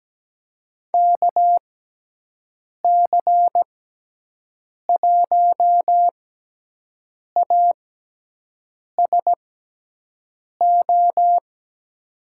Morse code, heard as a keyed tone, about 17 wpm, spaced as for 6 wpm, 700 Hz. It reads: KC1ASO
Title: USA Call Signs - 6 Characters - 6x Character Spacing - 17wpm